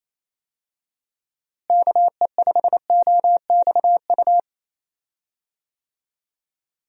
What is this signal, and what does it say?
Morse code, heard as a keyed tone, about 28 wpm, 700 Hz. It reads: KE5OXU
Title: USA Call Signs - 6 Characters - Repeated 3x - 28wpm